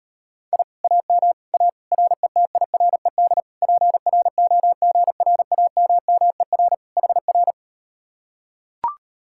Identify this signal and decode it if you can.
Morse code, heard as a keyed tone, about 39 wpm, 700 Hz. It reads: I AM A RETIRED PROGRAMMER HR